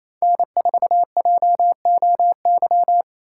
Morse code, heard as a keyed tone, about 33 wpm, 700 Hz. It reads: N4JOY
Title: Speed Racing - USA 1x3 Call Signs 28wpm